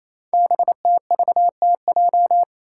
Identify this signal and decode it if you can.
Morse code, heard as a keyed tone, about 29 wpm, 700 Hz. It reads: BTVTJ
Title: Code Groups - 5 Letters - 28wpm